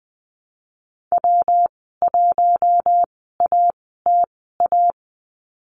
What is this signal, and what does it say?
Morse code, heard as a keyed tone, about 20 wpm, spaced as for 15 wpm, 700 Hz. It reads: W1ATA
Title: 122 Lesson 20g - Number 3 - Call Signs 20wpm